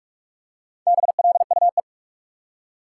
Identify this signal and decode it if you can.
Morse code, heard as a keyed tone, about 44 wpm, 700 Hz. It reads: BLUE
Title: Four Letter Words - 45wpm